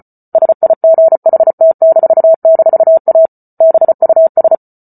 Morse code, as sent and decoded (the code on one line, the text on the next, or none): ... .. --. .... - -....- -....- .- / -... ..- ...
SIGHT--A BUS